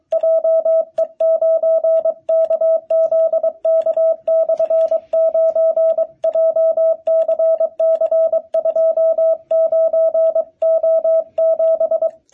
Morse code, heard as a keyed tone, about 22 wpm, 650 Hz, with radio noise. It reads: JE9KZK/9 JCC29O7
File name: jh8cbh / 2021cw